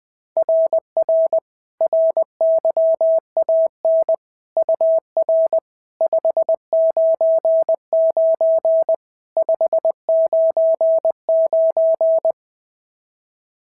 Morse code, heard as a keyed tone, about 19 wpm, 650 Hz. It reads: RR RYAN UR 599 599
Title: SOTA QSO Practice 20wpm